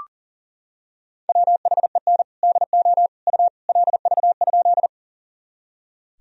Morse code, heard as a keyed tone, about 40 wpm, 700 Hz. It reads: WHEN DO U LV?
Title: Sentences from Top 400 Words and Abbreviations 40wpm